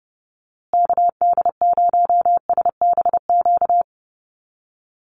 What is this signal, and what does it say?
Morse code, heard as a keyed tone, about 30 wpm, 700 Hz. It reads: KD0SBQ
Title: USA Call Signs Repeated 3x 30wpm